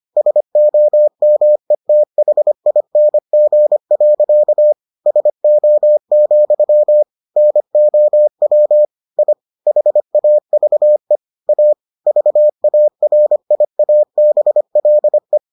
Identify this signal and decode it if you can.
Morse code, heard as a keyed tone, about 25 wpm, 600 Hz. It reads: SOMETHING. SO, NOW I HAVE A VARIABLE